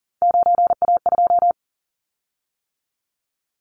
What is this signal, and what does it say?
Morse code, heard as a keyed tone, about 40 wpm, 700 Hz. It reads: 9A2
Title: Callsign Prefixes for Top 100 DXCC - 40wpm